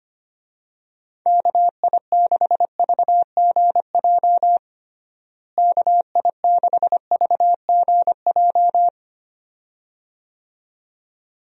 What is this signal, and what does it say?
Morse code, heard as a keyed tone, about 25 wpm, 700 Hz. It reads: KI6VGJ KI6VGJ
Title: USA Call Signs - 6 Characters - Repeated 2x - 25wpm